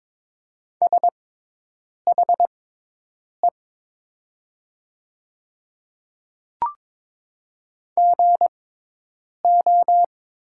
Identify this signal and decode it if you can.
Morse code, heard as a keyed tone, about 22 wpm, spaced as for 8 wpm, 700 Hz. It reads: SHE GO
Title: 6x Character Spacing - Top 100 Words - 22wpm